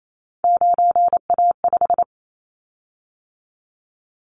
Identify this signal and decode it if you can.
Morse code, heard as a keyed tone, about 28 wpm, 700 Hz. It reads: 9A5
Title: Callsign Prefixes for Top 25 DXCC - 28wpm